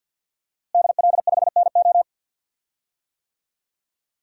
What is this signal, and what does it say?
Morse code, heard as a keyed tone, about 51 wpm, 700 Hz. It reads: DL5NO
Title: DXCC-100 Call Signs - 5 Characters - Repeated 3x - 50wpm